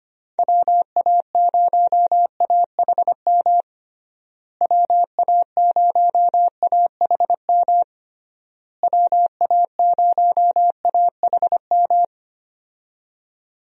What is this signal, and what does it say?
Morse code, heard as a keyed tone, about 25 wpm, 700 Hz. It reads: WA0AHM WA0AHM WA0AHM
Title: USA Call Signs - 6 Characters - Repeated 3x - 25wpm